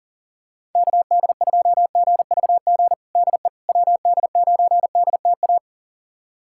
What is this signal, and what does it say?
Morse code, heard as a keyed tone, about 40 wpm, 700 Hz. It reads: KD2GVG DE WD9DTA
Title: USA Callsign DE Callsign - 6 Characters - 40wpm